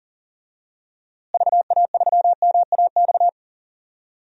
Morse code, heard as a keyed tone, about 39 wpm, 700 Hz. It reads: VA3MAX
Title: DXCC-100 Call Signs - 6 Characters - Repeated 2x - 40wpm